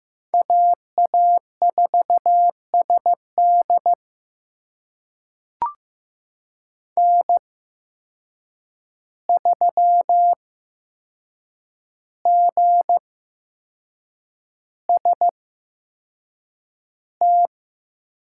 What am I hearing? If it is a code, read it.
Morse code, heard as a keyed tone, about 15 wpm, spaced as for 6 wpm, 700 Hz. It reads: AA4SD N3GST
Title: USA Call Signs - 5 Characters - 8x Character Spacing - 15wpm